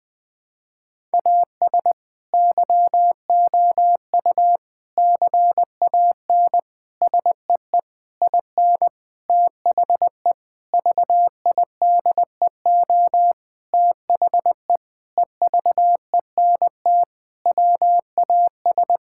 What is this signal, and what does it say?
Morse code, heard as a keyed tone, about 20 wpm, 700 Hz. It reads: AS YOU CAN SEE IN THE VIDEO THE EVENT WAS
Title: Sentences with 7-Word Phrases - 20wpm